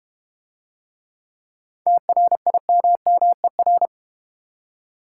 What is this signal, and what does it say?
Morse code, heard as a keyed tone, about 30 wpm, 690 Hz. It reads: TRIMMER
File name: practice / 25wpm / Difficult Words - Speed-Racing with Hints - 25wpm